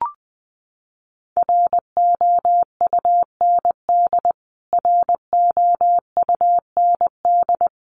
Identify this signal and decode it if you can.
Morse code, heard as a keyed tone, about 20 wpm, 700 Hz. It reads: ROUND ROUND
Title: Top 300 Words Repeated 20wpm